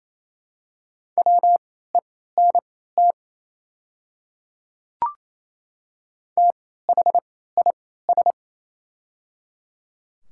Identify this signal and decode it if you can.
Morse code, heard as a keyed tone, about 28 wpm, spaced as for 16 wpm, 690 Hz. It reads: WENT THIS
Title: 3x Character Spacing - Top 200 Words - 28wpm